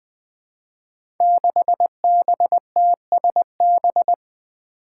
Morse code, heard as a keyed tone, about 20 wpm, 700 Hz. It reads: 6BTSB
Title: Five Challenging Letter-Number Combinations 20wpm